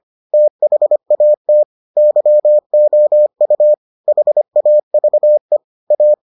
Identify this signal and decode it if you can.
Morse code, heard as a keyed tone, about 25 wpm, 600 Hz. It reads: THAT YOU HAVE A